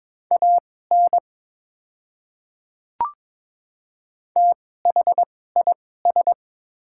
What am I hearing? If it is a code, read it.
Morse code, heard as a keyed tone, about 22 wpm, spaced as for 16 wpm, 690 Hz. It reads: AN THIS
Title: 2x Character Spacing - Top 100 Words - 22wpm